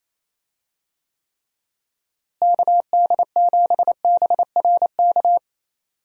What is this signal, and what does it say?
Morse code, heard as a keyed tone, about 28 wpm, 700 Hz. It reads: KD7BRK
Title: USA Call Signs - 6 Characters - Repeated 3x - 28wpm